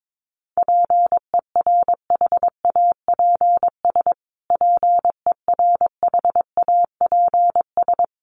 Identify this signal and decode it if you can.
Morse code, heard as a keyed tone, about 22 wpm, 700 Hz. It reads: PERHAPS PERHAPS
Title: Top 500 Words Repeated 22wpm